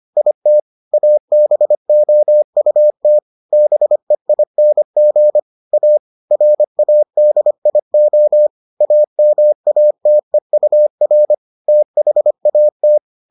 Morse code, heard as a keyed tone, about 25 wpm, 600 Hz. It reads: IT ABOUT BEING A RADIO AMATEUR THAT